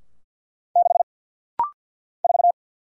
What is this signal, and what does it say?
Morse code, heard as a keyed tone, about 50 wpm, 700 Hz. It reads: 6 4
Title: Single Letter-Number Alt - Rapid-Fire 50wpm